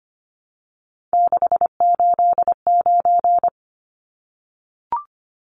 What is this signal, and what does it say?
Morse code, heard as a keyed tone, about 25 wpm, 700 Hz. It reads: 689
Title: Numbers - 3-Digit 25wpm